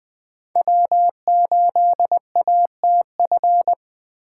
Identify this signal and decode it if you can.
Morse code, heard as a keyed tone, about 21 wpm, 700 Hz. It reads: W8ATF